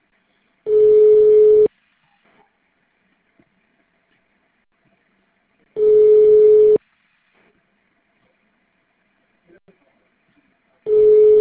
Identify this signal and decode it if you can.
Ringback tone (Europe, 425 Hz), with speech.